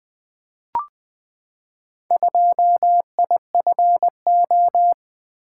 Morse code, heard as a keyed tone, about 20 wpm, 700 Hz. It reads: 2IFO
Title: Four Letter-Number Combinations - 20wpm